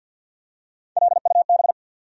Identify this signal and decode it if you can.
Morse code, heard as a keyed tone, about 47 wpm, 700 Hz. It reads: LUB
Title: USA Call Sign Postfixes - 3 Letters 50wpm